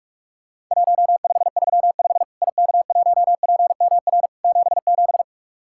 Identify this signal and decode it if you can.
Morse code, heard as a keyed tone, about 41 wpm, 700 Hz. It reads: 1535 IK1PMR 77
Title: Worked All Europe DX Contest - QTC Exchange - 45wpm